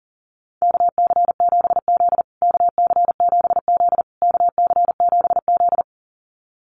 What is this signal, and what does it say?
Morse code, heard as a keyed tone, about 40 wpm, 700 Hz. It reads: KC7Z KC7Z KC7Z
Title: USA Call Signs Repeated 3x 40wpm